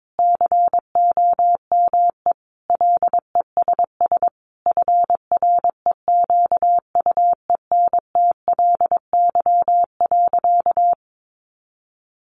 Morse code, heard as a keyed tone, about 22 wpm, 700 Hz. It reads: COME LESS FREQUENTLY.
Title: Sentences from Top 1300 Words 22wpm